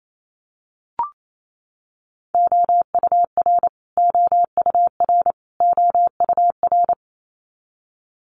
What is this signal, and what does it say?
Morse code, heard as a keyed tone, about 28 wpm, 700 Hz. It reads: OUR OUR OUR
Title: Top 300 Words Repeated 28wpm